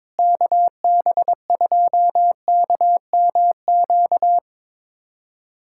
Morse code, heard as a keyed tone, about 22 wpm, 700 Hz. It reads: KB2KMQ
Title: USA Callsign DE Callsign - 6 Characters - 22wpm